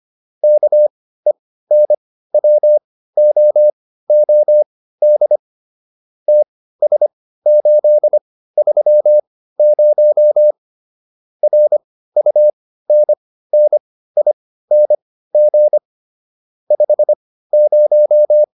Morse code, heard as a keyed tone, about 25 wpm, spaced as for 15 wpm, 600 Hz. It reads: KENWOOD TS830 RUNNING 50